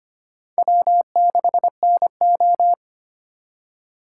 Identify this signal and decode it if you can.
Morse code, heard as a keyed tone, about 25 wpm, 700 Hz. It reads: W6NO
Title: USA Call Signs - 4 Characters - 3x Character Spacing - 25wpm